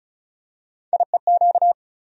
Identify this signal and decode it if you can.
Morse code, heard as a keyed tone, about 34 wpm, 700 Hz. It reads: IEQ
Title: USA Call Sign Postfixes - 3 Letters 35wpm